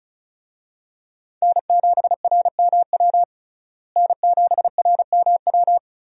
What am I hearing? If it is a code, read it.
Morse code, heard as a keyed tone, about 34 wpm, 700 Hz. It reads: N7RMW N7RMW
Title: USA Call Signs - 5 Characters - Repeated 3x - 35wpm